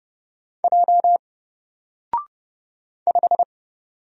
Morse code, heard as a keyed tone, about 30 wpm, 700 Hz. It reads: J 5
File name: Single-Letter-Number-30wpm